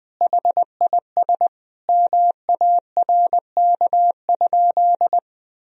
Morse code, heard as a keyed tone, about 20 wpm, 700 Hz. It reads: HIS MARK?
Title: Sentences from Top 1000 Words - Review Part 1 - 20wpm